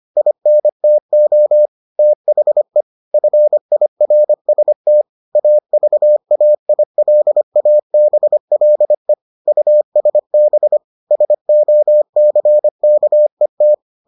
Morse code, heard as a keyed tone, about 25 wpm, 600 Hz. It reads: INTO THE FIRST AVAILABLE USB SOCKET